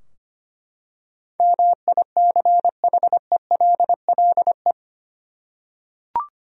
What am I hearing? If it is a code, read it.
Morse code, heard as a keyed tone, about 25 wpm, 700 Hz. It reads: MICHELLE
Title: USA Common Names - 25wpm